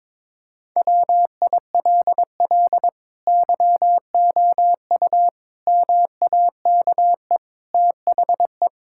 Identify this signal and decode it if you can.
Morse code, heard as a keyed tone, about 22 wpm, 700 Hz. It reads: WILL YOU MAKE THE
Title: Sentences from Top 1000 Words - Review Part 6 - 22wpm